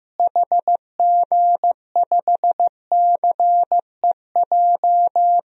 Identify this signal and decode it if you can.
Morse code, heard as a keyed tone, about 15 wpm, 700 Hz. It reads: HG5CEJ
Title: DXCC-100 Call Signs - 6 Characters - Repeated 3x - 15wpm